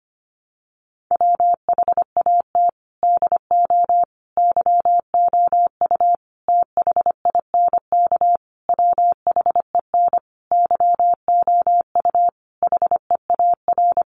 Morse code, heard as a keyed tone, about 25 wpm, 700 Hz. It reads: WHAT DO YOU THINK WHEN YOU HEAR
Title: Sentences from Top 300 Words 25wpm